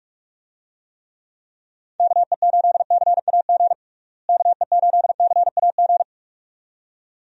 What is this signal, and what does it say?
Morse code, heard as a keyed tone, about 46 wpm, 700 Hz. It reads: KE8CAG KE8CAG
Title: USA Call Signs - 6 Characters - Repeated 2x - 45wpm